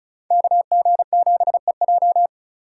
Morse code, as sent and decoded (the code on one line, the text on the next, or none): -.- --. --... .---
KG7EJ